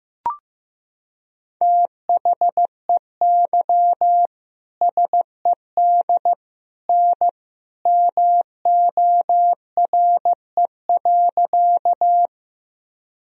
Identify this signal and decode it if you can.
Morse code, heard as a keyed tone, about 15 wpm, 700 Hz. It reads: THEY SED N MORE.
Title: Sentences from Top 100 Words and Abbreviations 15wpm